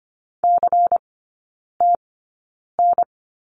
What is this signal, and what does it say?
Morse code, heard as a keyed tone, about 25 wpm, spaced as for 9 wpm, 700 Hz. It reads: CTN
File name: course / 25:10wpm / 093 Lesson 16e - Letter c - Three Characters 25wpm